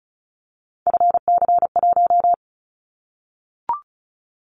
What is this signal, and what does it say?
Morse code, heard as a keyed tone, about 36 wpm, 700 Hz. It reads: FC1
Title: Callsign Prefixes for Top 50 DXCC - 35wpm